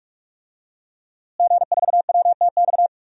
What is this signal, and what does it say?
Morse code, heard as a keyed tone, about 47 wpm, 700 Hz. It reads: G4WTX
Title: DXCC-100 Call Signs - 5 Characters - Repeated 3x - 45wpm